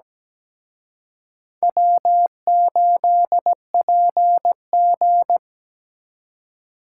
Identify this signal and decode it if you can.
Morse code, heard as a keyed tone, about 20 wpm, 700 Hz. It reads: W8PG
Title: Speed Racing - USA 1x2 Call Signs 17wpm